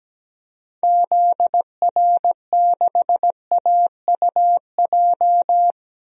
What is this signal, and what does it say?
Morse code, heard as a keyed tone, about 17 wpm, 700 Hz. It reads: ZR6AUJ
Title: DXCC-100 Call Signs - 6 Characters - 17wpm